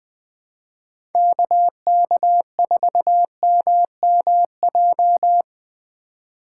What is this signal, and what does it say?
Morse code, heard as a keyed tone, about 20 wpm, 700 Hz. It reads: KK4MMJ
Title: USA Call Signs - 6 Characters - 4x Character Spacing - 20wpm